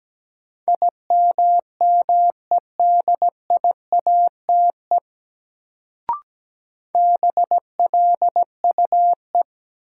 Morse code, heard as a keyed tone, about 17 wpm, 700 Hz. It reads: IMMEDIATE BLUE
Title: Sentences from Top 1200 Words - New Word Drill Alt - 17wpm